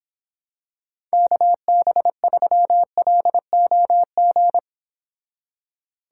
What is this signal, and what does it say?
Morse code, heard as a keyed tone, about 26 wpm, 700 Hz. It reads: KB3LOG